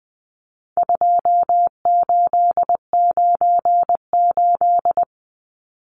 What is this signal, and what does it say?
Morse code, heard as a keyed tone, about 20 wpm, 700 Hz. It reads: 2898
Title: Numbers - 4-Digit 20wpm